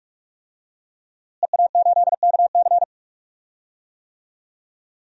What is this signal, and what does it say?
Morse code, heard as a keyed tone, about 41 wpm, 700 Hz. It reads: EA8KC
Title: DXCC-100 Call Signs - 5 Characters - Repeated 2x - 45wpm